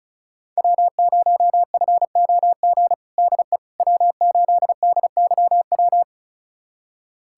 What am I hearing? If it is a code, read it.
Morse code, heard as a keyed tone, about 36 wpm, 700 Hz. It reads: W0FOG DE W8DYW
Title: USA Callsign DE Callsign - 5 Characters - 35wpm